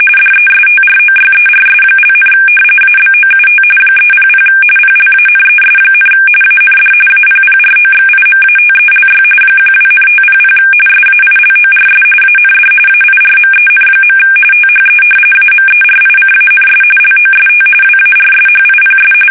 RTTY, about 45.45 baud. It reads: DESTROYERS), ABOUT 300 PLANES, AND APPROXIMATELY 12,500 SAILORS AND AIRMEN. THE AMERICANS LOST WARSHIPS
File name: fsk850 / battleshipnj